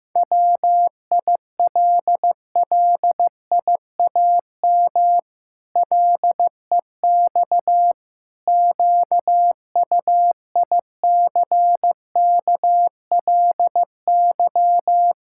Morse code, heard as a keyed tone, about 15 wpm, 700 Hz. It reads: WILLIAM LEX QUICKLY
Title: Pangram Challenge v2 - 15wpm